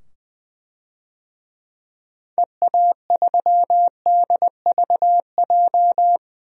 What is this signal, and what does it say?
Morse code, heard as a keyed tone, about 20 wpm, 700 Hz. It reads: EA3DVJ
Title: DXCC-100 Call Signs - 6 Characters - 20wpm